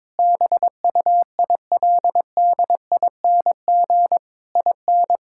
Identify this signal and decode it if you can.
Morse code, heard as a keyed tone, about 22 wpm, 700 Hz. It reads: BUILDING IN
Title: Sentences from Top 1000 Words - Review Part 5 - 22wpm